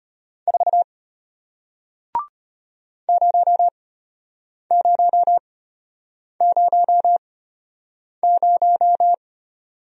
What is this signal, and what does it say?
Morse code, heard as a keyed tone, about 33 wpm, spaced as for 8 wpm, 700 Hz. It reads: 4 0000